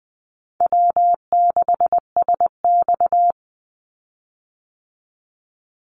Morse code, heard as a keyed tone, about 20 wpm, 700 Hz. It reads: W6SX
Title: Easy Call Signs Repeated 3x - 20wpm